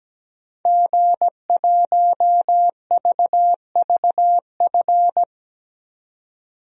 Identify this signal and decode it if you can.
Morse code, heard as a keyed tone, about 17 wpm, 700 Hz. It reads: G1VVF
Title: DXCC-100 Call Signs - 5 Characters - 17wpm